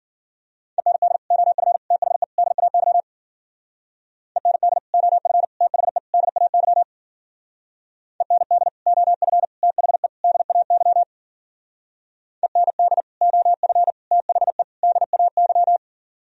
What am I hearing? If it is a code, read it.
Morse code, heard as a keyed tone, about 48 wpm, 700 Hz. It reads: END OF THE DAY END OF THE DAY END OF THE DAY END OF THE DAY
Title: Speed-Racing - Sets of 4 Words 40wpm